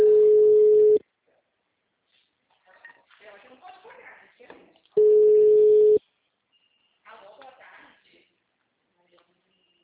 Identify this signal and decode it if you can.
Ringback tone (Europe, 425 Hz), with speech.
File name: out- -2010- - - .28496